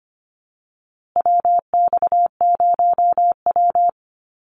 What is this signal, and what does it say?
Morse code, heard as a keyed tone, about 25 wpm, 700 Hz. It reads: WX0W
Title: USA Call Signs Repeated 2x 25wpm